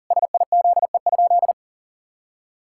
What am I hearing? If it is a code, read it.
Morse code, heard as a keyed tone, about 37 wpm, 690 Hz. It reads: SIZE?
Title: Sentences from Top 1000 Words - Review Part 6 - 40wpm